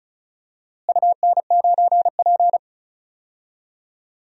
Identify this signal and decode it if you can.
Morse code, heard as a keyed tone, about 36 wpm, 700 Hz. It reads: UN9P
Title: DXCC-100 Call Signs - 4 Characters - Repeated 2x - 35wpm